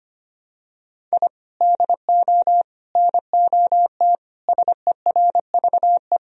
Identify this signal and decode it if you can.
Morse code, heard as a keyed tone, about 25 wpm, 700 Hz. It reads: I DO NOT SERVE
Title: Sentences from Top 1000 Words - Review Part 2 - 25wpm